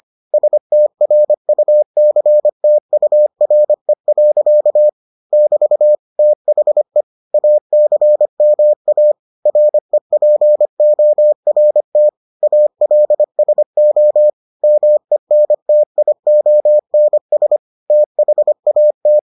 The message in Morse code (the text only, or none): STRUCTURE. <BT> THE ACMA REPORT ALSO MENTIONS THAT